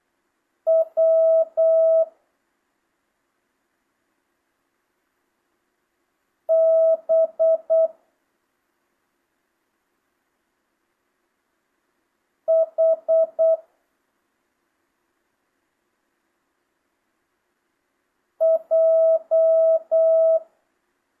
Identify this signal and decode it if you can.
Morse code, heard as a keyed tone, about 8 wpm, 650 Hz. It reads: W B H J
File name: jh8cbh / 2025cw